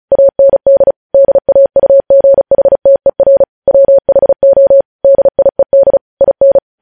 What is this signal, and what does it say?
Morse code, heard as a keyed tone, about 36 wpm, 550 Hz. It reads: AND DAUGHTER WHO DIED IN
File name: podcast / News headlines-1x-35x35